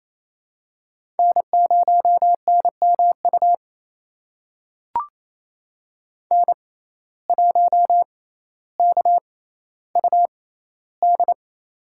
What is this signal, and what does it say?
Morse code, heard as a keyed tone, about 28 wpm, spaced as for 14 wpm, 700 Hz. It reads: N0NMU N1KUD
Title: USA Call Signs - 5 Characters - 6x Character Spacing - 28wpm